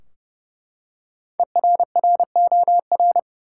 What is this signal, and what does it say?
Morse code, heard as a keyed tone, about 30 wpm, 700 Hz. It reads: ERROR
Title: Five Letter Words - 30wpm